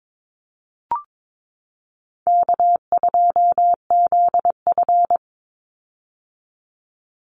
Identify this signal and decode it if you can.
Morse code, heard as a keyed tone, about 22 wpm, 700 Hz. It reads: K2ZF
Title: Easy Call Signs - 22wpm